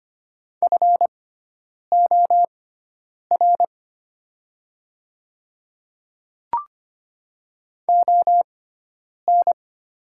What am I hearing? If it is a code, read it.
Morse code, heard as a keyed tone, about 25 wpm, spaced as for 9 wpm, 700 Hz. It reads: FOR ON
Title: 6x Character Spacing - Top 100 Words - 25wpm